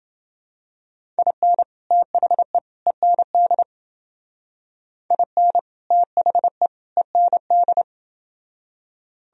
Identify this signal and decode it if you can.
Morse code, heard as a keyed tone, about 29 wpm, 700 Hz. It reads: IN THE END IN THE END